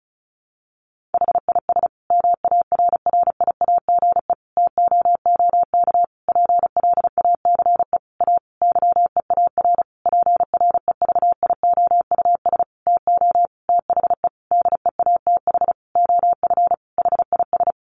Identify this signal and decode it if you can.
Morse code, heard as a keyed tone, about 34 wpm, 700 Hz. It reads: HIS MARRIAGE TOOK PLACE A YEAR PREVIOUS TO THE DEATH OF HIS
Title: Sentences from Top 1500 Words 35wpm